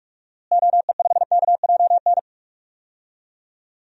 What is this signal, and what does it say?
Morse code, heard as a keyed tone, about 46 wpm, 700 Hz. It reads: OE5KJN
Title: DXCC-100 Call Signs - 6 Characters - Repeated 3x - 45wpm